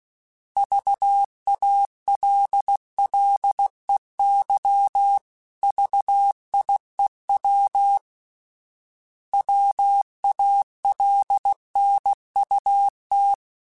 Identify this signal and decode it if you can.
Morse code, heard as a keyed tone, about 16 wpm, 800 Hz. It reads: VALLEY VIEW WALNUT
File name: fromSMs / FromSMs mp3s / Cities@15